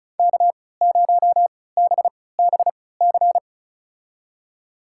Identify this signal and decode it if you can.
Morse code, heard as a keyed tone, about 34 wpm, spaced as for 20 wpm, 700 Hz. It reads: K0BBC
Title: USA Call Signs - 5 Characters - 3x Character Spacing - 35wpm